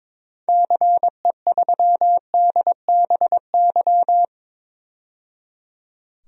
Morse code, heard as a keyed tone, about 22 wpm, 700 Hz. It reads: CE3DBY
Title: DXCC-100 Call Signs - 6 Characters - 22wpm